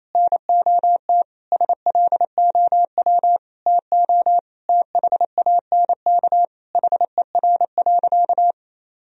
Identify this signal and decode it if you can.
Morse code, heard as a keyed tone, about 28 wpm, 700 Hz. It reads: NOT SLOW TO THANK HER.
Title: Sentences from Top 1000 Words - Review Part 2 - 28wpm